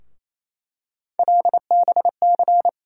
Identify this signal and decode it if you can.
Morse code, heard as a keyed tone, about 28 wpm, 700 Hz. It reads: LBC
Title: USA Call Sign Postfixes - 3 Letters 28wpm